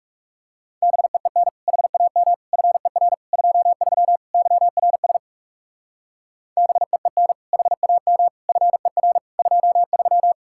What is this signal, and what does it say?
Morse code, heard as a keyed tone, about 38 wpm, 700 Hz. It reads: BEEN HAM FER 23 YRS BEEN HAM FER 23